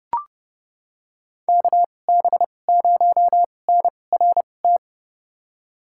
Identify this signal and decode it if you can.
Morse code, heard as a keyed tone, about 30 wpm, spaced as for 22 wpm, 700 Hz. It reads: KB0NRT
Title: USA Call Signs - 6 Characters - 2x Character Spacing - 30wpm